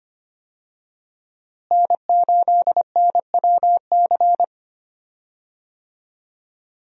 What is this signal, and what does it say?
Morse code, heard as a keyed tone, about 25 wpm, 700 Hz. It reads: N8NWC
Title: USA Call Signs - 5 Characters - Repeated 3x - 25wpm